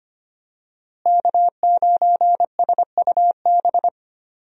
Morse code, heard as a keyed tone, about 25 wpm, 700 Hz. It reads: K9SUB
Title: USA Call Signs - 5 Characters - Repeated 2x - 25wpm